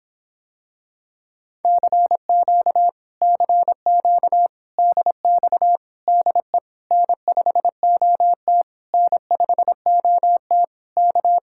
Morse code, heard as a keyed tone, about 26 wpm, 700 Hz. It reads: CQ CQ DX DE N5OT N5OT K